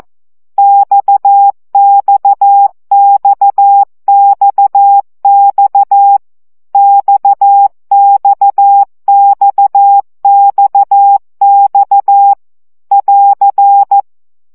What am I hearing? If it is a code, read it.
Morse code, heard as a keyed tone, about 14 wpm, 800 Hz. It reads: XXXXX XXXXX <AR>
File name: CW gyakorlas x 12 baud